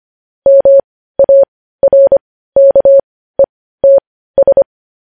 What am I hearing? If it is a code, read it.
Morse code, heard as a keyed tone, about 25 wpm, spaced as for 15 wpm, 550 Hz. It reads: MARKETS
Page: news headlines in morse code at 15 wpm